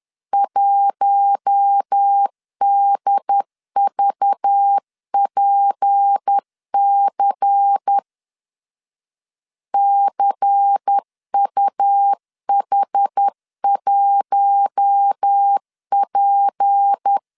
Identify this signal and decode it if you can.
Morse code, heard as a keyed tone, about 10 wpm, 800 Hz. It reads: 1DVPC CUH1P
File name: LEZIONI / 6 PKX0 / 6 P 10wpm 2432